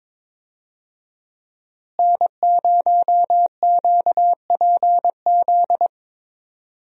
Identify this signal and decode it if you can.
Morse code, heard as a keyed tone, about 22 wpm, 700 Hz. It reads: N0QPZ
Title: USA Call Signs - 5 Characters - Repeated 3x - 22wpm